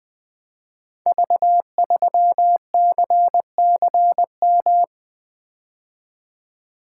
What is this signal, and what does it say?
Morse code, heard as a keyed tone, about 20 wpm, 700 Hz. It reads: V3CCM
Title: Code Groups - 5 Letter:Numbers - 20wpm